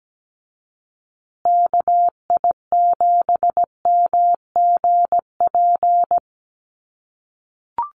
Morse code, heard as a keyed tone, about 17 wpm, 700 Hz. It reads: KI7MGP
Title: USA Call Signs Repeated 2x 17wpm